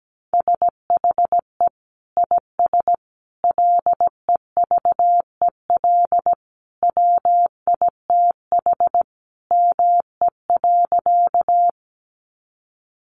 Morse code, heard as a keyed tone, about 17 wpm, 700 Hz. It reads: SHE IS LEVEL WITH ME.
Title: Sentences from Top 700 Words 17wpm